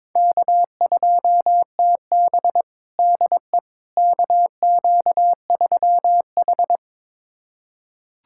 Morse code, heard as a keyed tone, about 21 wpm, 700 Hz. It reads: K2TB DE KQ3H
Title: USA Callsign DE Callsign - 4 Characters - 22wpm